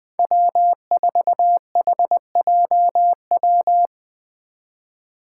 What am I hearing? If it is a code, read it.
Morse code, heard as a keyed tone, about 20 wpm, 700 Hz. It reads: W4HJW
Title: Five Challenging Letter-Number Combinations 20wpm